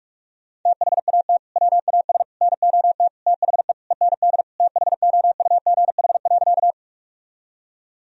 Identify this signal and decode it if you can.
Morse code, heard as a keyed tone, about 46 wpm, 700 Hz. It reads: THAT WAS NOT THE END THOUGH.
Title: Sentences from Top 1000 Words - Review Part 2 - 45wpm